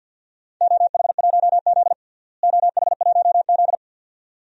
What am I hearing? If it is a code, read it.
Morse code, heard as a keyed tone, about 49 wpm, 700 Hz. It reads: OH1Z OH1Z
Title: DXCC-100 Call Signs - 4 Characters - Repeated 2x - 50wpm